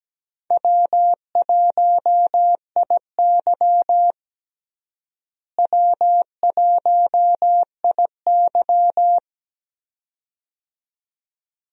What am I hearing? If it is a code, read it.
Morse code, heard as a keyed tone, about 17 wpm, 700 Hz. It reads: W1IY W1IY
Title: USA Call Signs - 4 Characters - Repeated 2x - 17wpm